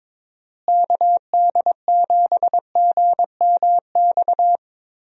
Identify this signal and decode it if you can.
Morse code, heard as a keyed tone, about 22 wpm, 700 Hz. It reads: KD7GMX
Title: USA Call Signs - 6 Characters - Repeated 3x - 22wpm